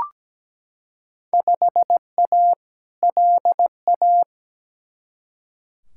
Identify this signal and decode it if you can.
Morse code, heard as a keyed tone, about 17 wpm, 700 Hz. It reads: 5A LA
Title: Field Day Exchanges Encore - 17wpm